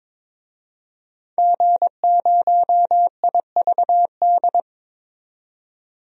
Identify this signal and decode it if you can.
Morse code, heard as a keyed tone, about 21 wpm, 700 Hz. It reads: G0IVD
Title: DXCC-100 Call Signs - 5 Characters - 22wpm